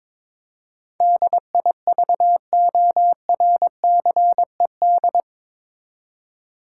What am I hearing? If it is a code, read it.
Morse code, heard as a keyed tone, about 22 wpm, 700 Hz. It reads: DIVORCED
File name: practice / 22wpm / Eight Letter Words - 22wpm